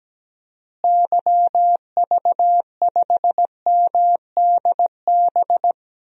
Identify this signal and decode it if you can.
Morse code, heard as a keyed tone, about 17 wpm, 700 Hz. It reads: YV5MDB
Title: DXCC-100 Call Signs - 6 Characters - Repeated 2x - 17wpm